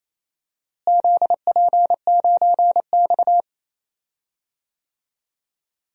Morse code, heard as a keyed tone, about 28 wpm, 700 Hz. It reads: ZP9X
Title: DXCC-100 Call Signs - 4 Characters - Repeated 3x - 28wpm